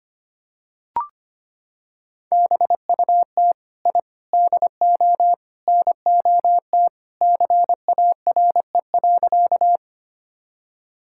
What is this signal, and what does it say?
Morse code, heard as a keyed tone, about 25 wpm, 700 Hz. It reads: BUT I DO NOT CARE.
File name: practice / 25wpm / Sentences with 4-Word Phrases - 25wpm